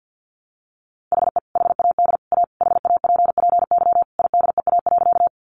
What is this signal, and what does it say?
Morse code, heard as a keyed tone, about 50 wpm, 700 Hz. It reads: HE HAD A HAPPY IDEA.
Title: Sentences from Top 1000 Words 50wpm